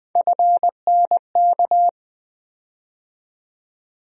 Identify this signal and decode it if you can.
Morse code, heard as a keyed tone, about 20 wpm, 700 Hz. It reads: FNK
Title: USA Call Sign Postfixes - 3 Letters 20wpm